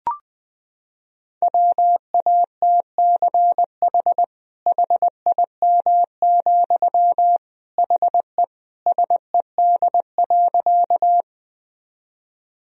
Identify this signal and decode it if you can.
Morse code, heard as a keyed tone, about 21 wpm, 700 Hz. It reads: WATCH HIM, HE SED.
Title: Sentences from Top 300 Words and Abbreviations 20wpm